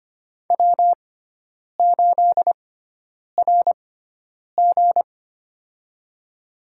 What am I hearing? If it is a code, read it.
Morse code, heard as a keyed tone, about 25 wpm, spaced as for 9 wpm, 700 Hz. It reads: W8RG
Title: USA Call Signs - 4 Characters - 6x Character Spacing - 25wpm